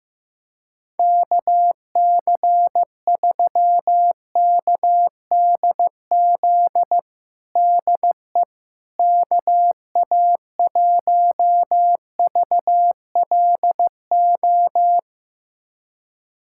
Morse code, heard as a keyed tone, about 15 wpm, 700 Hz. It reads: KC3KDZ DE KA1VLO
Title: USA Callsign DE Callsign - 6 Characters - 15wpm